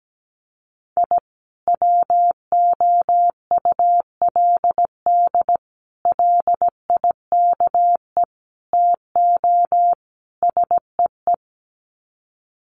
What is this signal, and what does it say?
Morse code, heard as a keyed tone, about 17 wpm, 700 Hz. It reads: I WOULD LIKE TO SEE
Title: Sets of 5 Words 17wpm